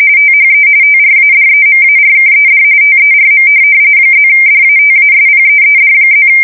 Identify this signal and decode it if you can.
RTTY, about 45.45 baud. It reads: THE TRADEMARKED TERM +DATASPEED+
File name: fsk170 / teletype-corp